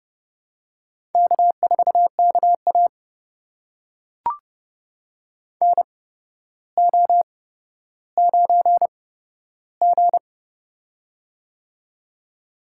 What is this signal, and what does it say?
Morse code, heard as a keyed tone, about 30 wpm, spaced as for 13 wpm, 700 Hz. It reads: K4KA NO9G
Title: USA Call Signs - 4 Characters - 8x Character Spacing - 30wpm